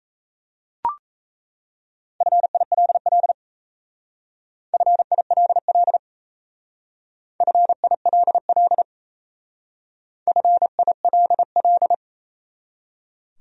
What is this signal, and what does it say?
Morse code, heard as a keyed tone, about 34 wpm, 700 Hz. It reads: FILL FILL FILL FILL
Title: Speed-Racing Top 500 Words 28wpm